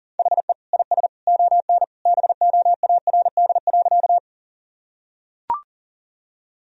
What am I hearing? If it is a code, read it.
Morse code, heard as a keyed tone, about 40 wpm, 700 Hz. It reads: HE IS ON BOARD.
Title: Sentences from Top 1000 Words - Review Part 4 - 40wpm